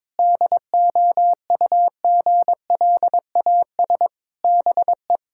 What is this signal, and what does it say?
Morse code, heard as a keyed tone, about 22 wpm, 700 Hz. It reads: DOUGLAS BE
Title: Pangram Challenge v2 - 22wpm